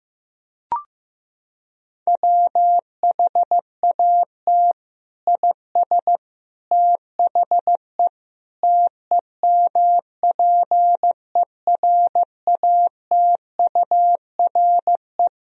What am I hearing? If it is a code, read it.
Morse code, heard as a keyed tone, about 15 wpm, 700 Hz. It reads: WHAT IS THE TEMPERATURE
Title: Sentences from Top 1000 Words - Review Part 4 - 15wpm